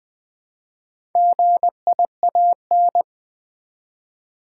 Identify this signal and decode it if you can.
Morse code, heard as a keyed tone, about 21 wpm, 700 Hz. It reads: GIAN
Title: Build A Word - 1 Letter At A Time - 5-Letter Words - 20wpm